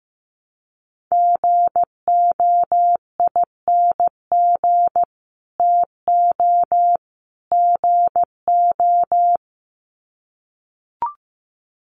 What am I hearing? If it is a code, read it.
Morse code, heard as a keyed tone, about 15 wpm, 700 Hz. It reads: GOING TO GO
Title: Sets of 3 Words 15wpm